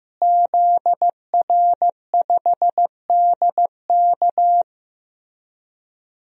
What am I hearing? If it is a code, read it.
Morse code, heard as a keyed tone, about 15 wpm, 700 Hz. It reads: ZR5DK
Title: DXCC-100 Call Signs - 5 Characters - Repeated 2x - 15wpm